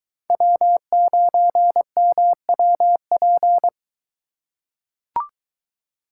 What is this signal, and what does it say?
Morse code, heard as a keyed tone, about 22 wpm, 700 Hz. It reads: W9MWP